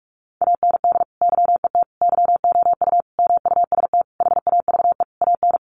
Morse code, heard as a keyed tone, about 44 wpm, 700 Hz. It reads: AND YET YOU MUST HAVE AN
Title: Sentences from Top 1300 Words 45wpm